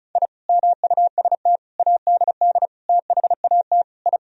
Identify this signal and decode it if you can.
Morse code, heard as a keyed tone, about 34 wpm, 700 Hz. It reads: I MUST ADD THAT I
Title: Sentences from Top 1000 Words - Review Part 1 - 35wpm